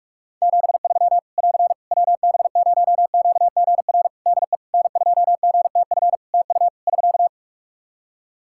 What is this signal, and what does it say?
Morse code, heard as a keyed tone, about 44 wpm, 700 Hz. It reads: 73 <AR> WB0QGR DE N2GTF TU <SK>
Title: Practice QSO - Exchange 4 - Ending - 45wpm